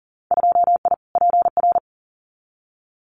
Morse code, heard as a keyed tone, about 40 wpm, 700 Hz. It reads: 2I PR
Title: Winter Field Day Exchanges Encore 40wpm